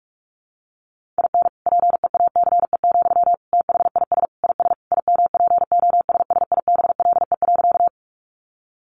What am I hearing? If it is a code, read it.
Morse code, heard as a keyed tone, about 44 wpm, 700 Hz. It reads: IN PEACE, THIS IS IMPOSSIBLE.
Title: Sentences from Top 1500 Words 45wpm